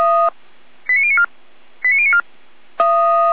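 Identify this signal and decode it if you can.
Five-tone selective call (ZVEI), heard as quick runs of five tones, tones of about 70 ms. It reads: 89094 89094
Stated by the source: ZVEI five-tone selective call received with an SDR.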